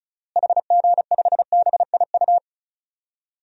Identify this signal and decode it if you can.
Morse code, heard as a keyed tone, about 34 wpm, 700 Hz. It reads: HG5BIU
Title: DXCC-100 Call Signs - 6 Characters - Repeated 3x - 35wpm